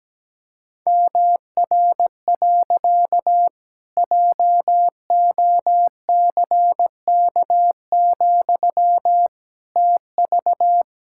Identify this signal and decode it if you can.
Morse code, heard as a keyed tone, about 17 wpm, 700 Hz. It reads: MR. JOCK, TV
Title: Pangram Challenge v2 - 17wpm